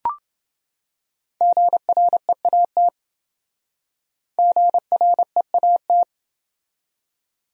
Morse code, heard as a keyed tone, about 28 wpm, 700 Hz. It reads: GREAT GREAT